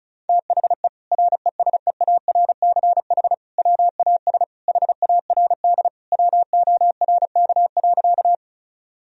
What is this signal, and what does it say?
Morse code, heard as a keyed tone, about 34 wpm, 700 Hz. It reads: THE RESEARCH WAS HARD WORK.
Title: Sentences from Top 1000 Words - Review Part 4 - 35wpm